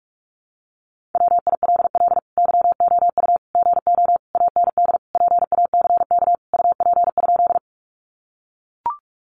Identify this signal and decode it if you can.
Morse code, heard as a keyed tone, about 44 wpm, 700 Hz. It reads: WILL YOU GO AND PACK UP?
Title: Sentences from Top 1300 Words 45wpm